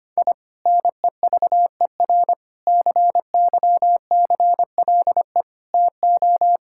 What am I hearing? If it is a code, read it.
Morse code, heard as a keyed tone, about 25 wpm, 700 Hz. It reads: I NEVER CYCLE TO
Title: Sentences from Top 1000 Words - Review Part 6 - 25wpm